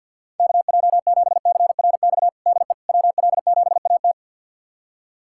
Morse code, heard as a keyed tone, about 51 wpm, 700 Hz. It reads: KJ7CRX DE WL7AT
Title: USA Callsign DE Callsign 50wpm